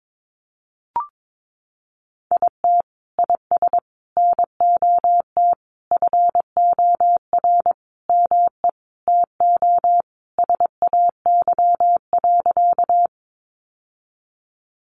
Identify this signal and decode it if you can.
Morse code, heard as a keyed tone, about 22 wpm, 700 Hz. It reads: IT IS NOT FOR ME TO SAY.
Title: Sentences from Top 600 Words 22wpm